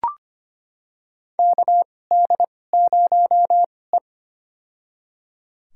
Morse code, heard as a keyed tone, about 25 wpm, spaced as for 18 wpm, 700 Hz. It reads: KD0E
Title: USA Call Signs - 4 Characters - 2x Character Spacing - 25wpm